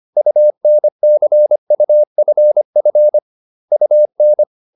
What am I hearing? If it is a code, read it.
Morse code, heard as a keyed tone, about 25 wpm, 600 Hz. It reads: UNCUFF UN